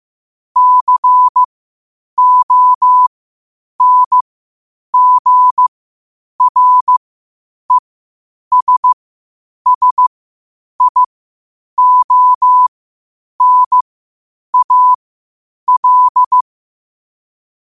Morse code, heard as a keyed tone, about 15 wpm, spaced as for 8 wpm, 1000 Hz. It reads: CONGRESSIONAL
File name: qotdfiles / 8wpm